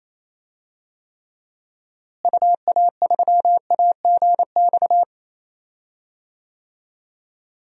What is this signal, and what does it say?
Morse code, heard as a keyed tone, about 28 wpm, 700 Hz. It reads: UA3AGX
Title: DXCC-100 Call Signs - 6 Characters - Repeated 3x - 28wpm